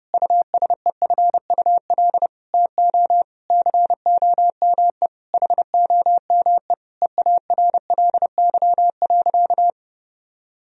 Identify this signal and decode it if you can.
Morse code, heard as a keyed tone, about 30 wpm, 700 Hz. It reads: USEFUL TO COME HOME EARLY.
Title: Sentences from Top 1000 Words - Review Part 4 - 30wpm